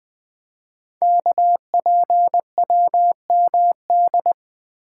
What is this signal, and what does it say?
Morse code, heard as a keyed tone, about 20 wpm, 700 Hz. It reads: KPWMD
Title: Code Groups - 5 Letters - 20wpm